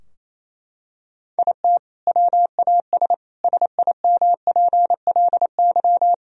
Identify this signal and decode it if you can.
Morse code, heard as a keyed tone, about 28 wpm, 700 Hz. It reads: IT WAS SIMPLY
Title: Sentences from Top 1000 Words - Review Part 4 - 28wpm